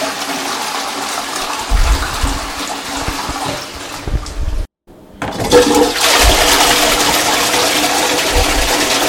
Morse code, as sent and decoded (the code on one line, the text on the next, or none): none